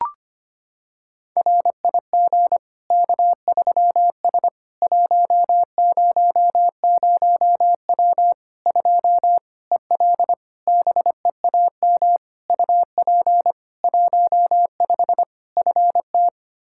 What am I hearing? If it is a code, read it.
Morse code, heard as a keyed tone, about 25 wpm, 700 Hz. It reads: RIG K3S 100W 2 EL BEAM UP 15 FT
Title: Practice QSO - Exchange 2 - Info - 25wpm